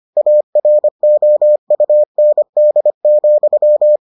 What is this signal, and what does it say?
Morse code, heard as a keyed tone, about 25 wpm, 600 Hz. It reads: AROUND,